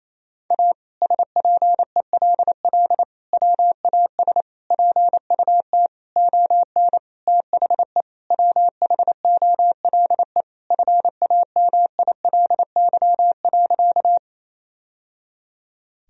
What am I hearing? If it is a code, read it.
Morse code, heard as a keyed tone, about 28 wpm, 700 Hz. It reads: A SPELL WAS PUT ON THE WHOLE FAMILY.
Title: Sentences from Top 1000 Words - Review Part 1 - 28wpm